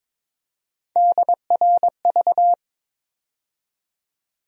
Morse code, heard as a keyed tone, about 22 wpm, 700 Hz. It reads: DRV
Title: USA Call Sign Postfixes - 3 Letters 22wpm